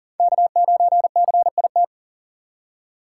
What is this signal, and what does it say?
Morse code, heard as a keyed tone, about 40 wpm, 700 Hz. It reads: K9CIT
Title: USA Call Signs - 5 Characters - 6x Character Spacing - 40wpm